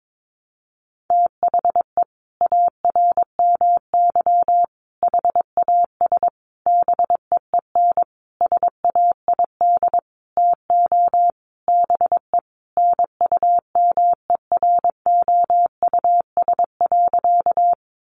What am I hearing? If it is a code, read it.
Morse code, heard as a keyed tone, about 22 wpm, 700 Hz. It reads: THE ARMY HAS BEEN SAID TO BE NUMEROUS.
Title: Sentences from Top 1400 Words 22wpm